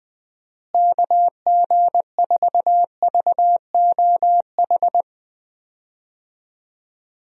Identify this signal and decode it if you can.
Morse code, heard as a keyed tone, about 20 wpm, 700 Hz. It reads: KG4VOH
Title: USA Call Signs - 6 Characters - Repeated 3x - 20wpm